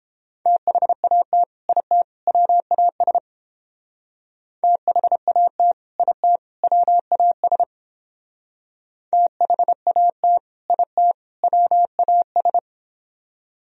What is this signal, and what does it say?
Morse code, heard as a keyed tone, about 30 wpm, 700 Hz. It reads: THAT IT WAS THAT IT WAS THAT IT WAS